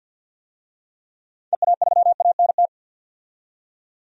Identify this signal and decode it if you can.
Morse code, heard as a keyed tone, about 52 wpm, 700 Hz. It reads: EA3ANT
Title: DXCC-100 Call Signs - 6 Characters - 50wpm